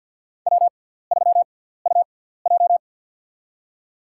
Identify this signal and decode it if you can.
Morse code, heard as a keyed tone, about 52 wpm, spaced as for 17 wpm, 700 Hz. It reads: W3UJ
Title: USA Call Signs - 4 Characters - 6x Character Spacing - 50wpm